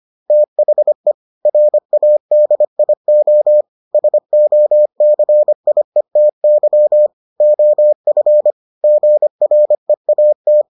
Morse code, heard as a keyed tone, about 25 wpm, 600 Hz. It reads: THE RADIO SOCIETY OF GREAT